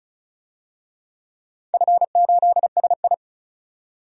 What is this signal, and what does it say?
Morse code, heard as a keyed tone, about 34 wpm, 700 Hz. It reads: F8SI
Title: DXCC-100 Call Signs - 4 Characters - 35wpm